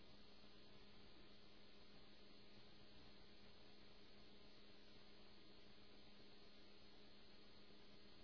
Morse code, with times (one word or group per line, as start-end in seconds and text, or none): none